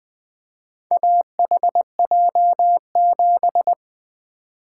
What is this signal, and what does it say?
Morse code, heard as a keyed tone, about 20 wpm, 700 Hz. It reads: AHJ7
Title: Four Letter-Number Combinations - 20wpm